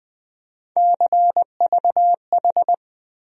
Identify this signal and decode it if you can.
Morse code, heard as a keyed tone, about 20 wpm, 700 Hz. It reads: CVH
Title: USA Call Sign Postfixes - 3 Letters 20wpm